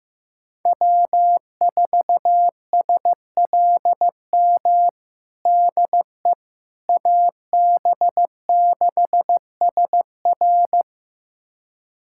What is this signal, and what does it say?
Morse code, heard as a keyed tone, about 15 wpm, 700 Hz. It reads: W4SLM DE AB6SR
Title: USA Callsign DE Callsign - 5 Characters - 15wpm